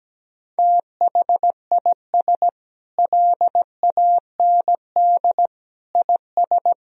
Morse code, heard as a keyed tone, about 17 wpm, 700 Hz. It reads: THIS LAND IS
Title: Sentences from Top 1000 Words - Review Part 1 - 17wpm